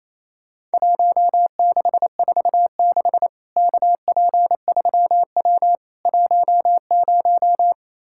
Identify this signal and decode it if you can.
Morse code, heard as a keyed tone, about 28 wpm, 700 Hz. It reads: 1646 KP3W 10
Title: Worked All Europe DX Contest - QTC Exchange - 28wpm